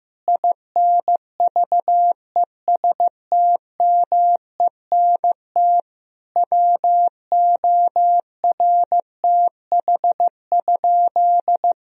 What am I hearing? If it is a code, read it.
Morse code, heard as a keyed tone, about 15 wpm, 700 Hz. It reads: INVESTMENT WORTH?
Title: Sentences from Top 1000 Words - Review Part 4 - 15wpm